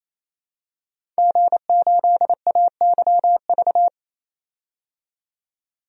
Morse code, heard as a keyed tone, about 28 wpm, 700 Hz. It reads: G8AYV
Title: DXCC-100 Call Signs - 5 Characters - 28wpm